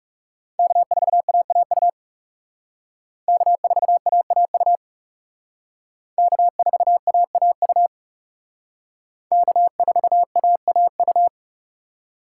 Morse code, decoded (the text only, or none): K4AAU K4AAU K4AAU K4AAU